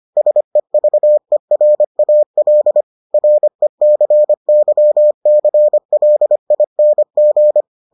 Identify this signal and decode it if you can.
Morse code, heard as a keyed tone, about 25 wpm, 600 Hz. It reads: SEVERAL RECYCLING